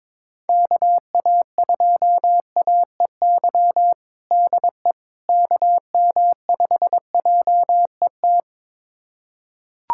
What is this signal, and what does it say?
Morse code, heard as a keyed tone, about 22 wpm, 700 Hz. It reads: KA2AEY DE KM5JET
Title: USA Callsign DE Callsign - 6 Characters - 22wpm